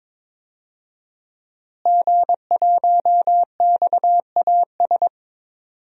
Morse code, heard as a keyed tone, about 22 wpm, 700 Hz. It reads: G1XAS
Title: DXCC-100 Call Signs - 5 Characters - Repeated 3x - 22wpm